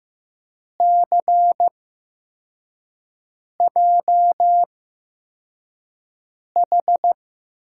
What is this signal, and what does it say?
Morse code, heard as a keyed tone, about 15 wpm, spaced as for 4 wpm, 700 Hz. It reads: CJH
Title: USA Call Signs - 8x Character Spacing - 15wpm